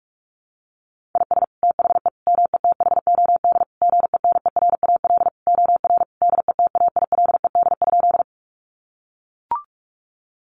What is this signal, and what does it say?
Morse code, heard as a keyed tone, about 44 wpm, 700 Hz. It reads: IS THE METHOD GENERAL OR DETAILED?
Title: Sentences from Top 1500 Words 45wpm